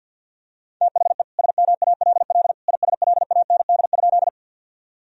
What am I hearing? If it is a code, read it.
Morse code, heard as a keyed tone, about 50 wpm, 700 Hz. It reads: THE SMALL ISLAND?
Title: Sentences from Top 1000 Words - Review Part 3 - 50wpm